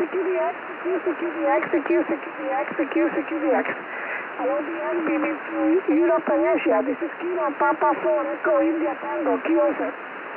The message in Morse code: none